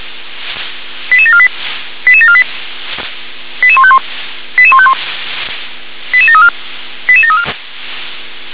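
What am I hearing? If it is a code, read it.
Five-tone selective call (ZVEI), heard as quick runs of five tones, tones of about 70 ms. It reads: 88748 88748 88151 88151 88734 88734
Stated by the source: Received on an SDR.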